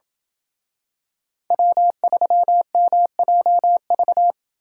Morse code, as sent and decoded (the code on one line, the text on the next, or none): .-- ...-- -- .--- ...-
W3MJV